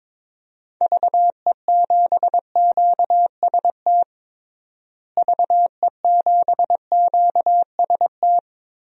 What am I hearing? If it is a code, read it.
Morse code, heard as a keyed tone, about 22 wpm, 700 Hz. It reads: VE7QST VE7QST
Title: DXCC-100 Call Signs - 6 Characters - Repeated 3x - 22wpm